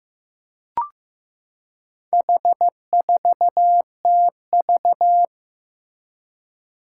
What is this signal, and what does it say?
Morse code, heard as a keyed tone, about 15 wpm, 700 Hz. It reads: H4TV
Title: Four Challenging Letter-Number Combinations 15wpm